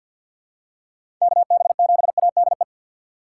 Morse code, heard as a keyed tone, about 53 wpm, 700 Hz. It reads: KB7ADE
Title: USA Call Signs - 6 Characters - 6x Character Spacing - 50wpm